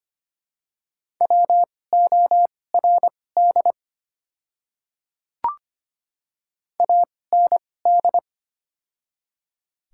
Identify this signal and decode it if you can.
Morse code, heard as a keyed tone, about 25 wpm, spaced as for 18 wpm, 700 Hz. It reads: WORD AND
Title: 2x Character Spacing - Top 300 Words - 25wpm